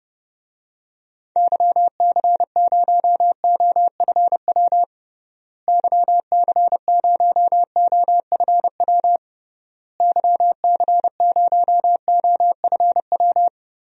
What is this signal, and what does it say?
Morse code, heard as a keyed tone, about 30 wpm, 700 Hz. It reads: YC0OFW YC0OFW YC0OFW
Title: DXCC-100 Call Signs - 6 Characters - Repeated 3x - 30wpm